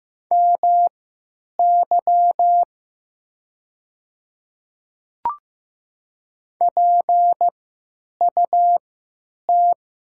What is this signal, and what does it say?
Morse code, heard as a keyed tone, about 15 wpm, spaced as for 8 wpm, 700 Hz. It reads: MY PUT
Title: 3x Character Spacing - Top 300 Words - 15wpm